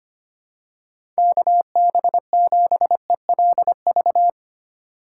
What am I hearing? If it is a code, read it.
Morse code, heard as a keyed tone, about 25 wpm, 700 Hz. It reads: KB7ELV
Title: USA Call Signs - 6 Characters - 4x Character Spacing - 25wpm